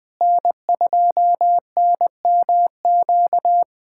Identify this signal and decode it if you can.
Morse code, heard as a keyed tone, about 20 wpm, 700 Hz. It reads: N2NMQ
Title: USA Callsign DE Callsign - 5 Characters - 20wpm